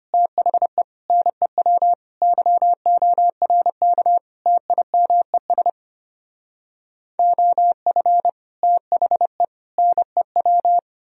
Morse code, heard as a keyed tone, about 32 wpm, 700 Hz. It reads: THE NEW YORK TIMES OF THE NEW